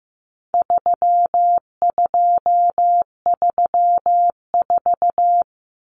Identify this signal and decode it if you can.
Morse code, heard as a keyed tone, about 15 wpm, 700 Hz. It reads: 3234
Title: Common Cut Numbers - 4-Digit 15wpm